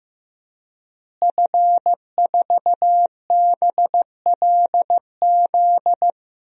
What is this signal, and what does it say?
Morse code, heard as a keyed tone, about 15 wpm, 700 Hz. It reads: F4BLZ
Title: DXCC-100 Call Signs - 5 Characters - Repeated 3x - 15wpm